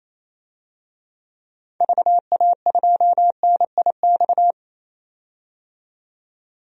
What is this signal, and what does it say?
Morse code, heard as a keyed tone, about 28 wpm, 700 Hz. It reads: VA2NIX
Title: DXCC-100 Call Signs - 6 Characters - 28wpm